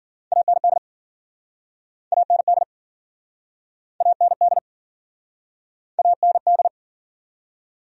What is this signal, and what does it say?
Morse code, heard as a keyed tone, about 52 wpm, 700 Hz. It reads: AND AND AND AND